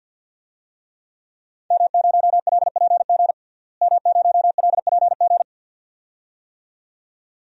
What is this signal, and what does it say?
Morse code, heard as a keyed tone, about 49 wpm, 700 Hz. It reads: M0LPG M0LPG
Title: DXCC-100 Call Signs - 5 Characters - Repeated 2x - 50wpm